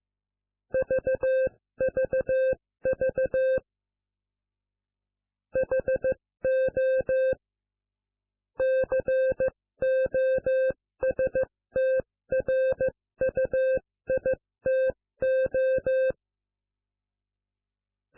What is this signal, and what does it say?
Morse code, heard as a keyed tone, about 15 wpm, 520 Hz. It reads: VVV HO COSTRUITO